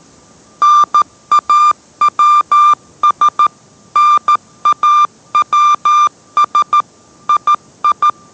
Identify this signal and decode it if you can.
Morse code, heard as a keyed tone, about 13 wpm, 1190 Hz. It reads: NAWS NAWS II